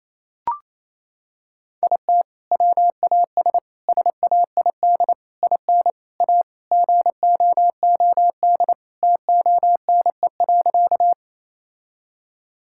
Morse code, heard as a keyed tone, about 28 wpm, 700 Hz. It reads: IT WAS SAID IN A GOOD TONE.
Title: Sentences from Top 1600 Words 28wpm